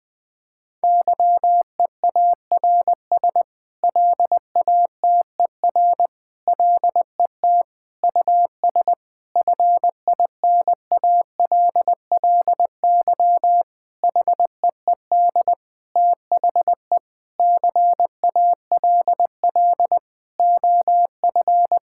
Morse code, heard as a keyed tone, about 20 wpm, 700 Hz. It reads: YEARS LATER LET US FINALLY HEED THE CALL OF
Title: Sentences with 8-Word Phrases - 20wpm